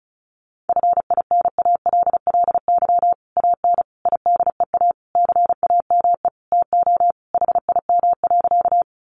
Morse code, heard as a keyed tone, about 34 wpm, 700 Hz. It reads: FINALLY AN IDEA CAME TO HIM.
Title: Sentences from Top 1000 Words 35wpm